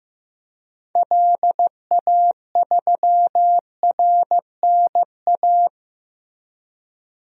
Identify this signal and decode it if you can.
Morse code, heard as a keyed tone, about 15 wpm, 700 Hz. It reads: LA3RNA
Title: DXCC-100 Call Signs - 6 Characters - Repeated 2x - 15wpm